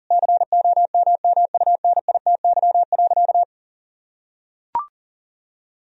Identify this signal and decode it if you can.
Morse code, heard as a keyed tone, about 40 wpm, 700 Hz. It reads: COMMUNITY.
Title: Sentences from Top 1000 Words - Review Part 4 - 40wpm